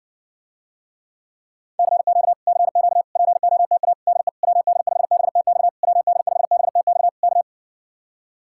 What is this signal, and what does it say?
Morse code, heard as a keyed tone, about 59 wpm, 700 Hz. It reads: CQ CQ POTA DE WD5BTX WD5BTX K